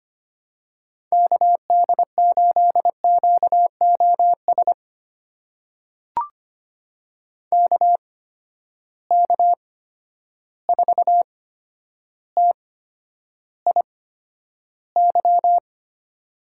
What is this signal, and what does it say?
Morse code, heard as a keyed tone, about 25 wpm, spaced as for 11 wpm, 700 Hz. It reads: KD8QOS KK4TIY
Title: USA Call Signs - 8x Character Spacing - 25wpm